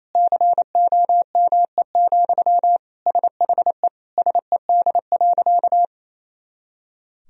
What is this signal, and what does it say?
Morse code, heard as a keyed tone, about 28 wpm, 700 Hz. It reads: COME, SHE SED.
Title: Sentences from Top 100 Words and Abbreviations 28wpm